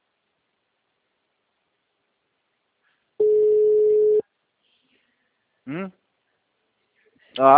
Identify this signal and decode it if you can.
Ringback tone (Europe, 425 Hz), with speech.